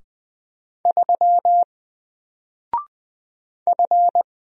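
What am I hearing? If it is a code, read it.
Morse code, heard as a keyed tone, about 20 wpm, 700 Hz. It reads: 3 F
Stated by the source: Single letter and number Morse drill at 20 wpm, from N4LCD.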